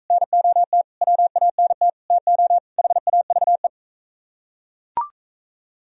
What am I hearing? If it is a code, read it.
Morse code, heard as a keyed tone, about 40 wpm, 700 Hz. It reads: NOT WANT TO HAVE